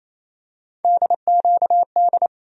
Morse code, heard as a keyed tone, about 28 wpm, 700 Hz. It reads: DQD
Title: USA Call Sign Postfixes - 3 Letters 28wpm